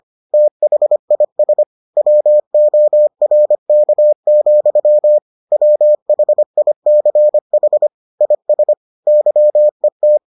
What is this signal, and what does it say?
Morse code, heard as a keyed tone, about 25 wpm, 600 Hz. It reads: THIS WORK, WHICH IS YET